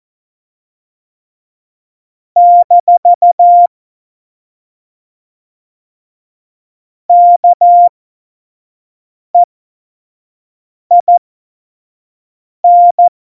Morse code, heard as a keyed tone, about 14 wpm, spaced as for 5 wpm, 700 Hz. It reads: - KEIN
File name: Morsekurs / Witze / Witz 17.Dezember 2019, 25 F70